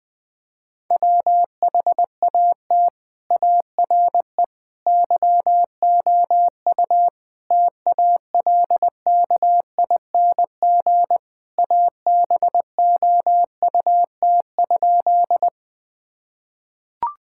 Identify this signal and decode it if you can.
Morse code, heard as a keyed tone, about 20 wpm, 700 Hz. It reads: WHAT ARE YOU TALKING ABOUT?
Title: Sentences with 5-Word Phrases - 20wpm